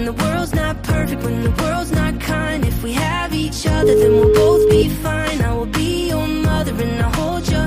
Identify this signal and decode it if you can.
Ringback tone (Europe, 425 Hz), with noise.